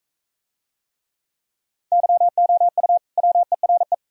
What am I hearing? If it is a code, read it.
Morse code, heard as a keyed tone, about 43 wpm, 700 Hz. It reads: YOU WERE